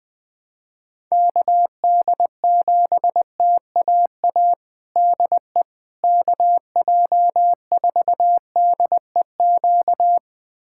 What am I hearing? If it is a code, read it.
Morse code, heard as a keyed tone, about 21 wpm, 700 Hz. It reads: KD7TAA DE KJ4DEQ
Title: USA Callsign DE Callsign 20wpm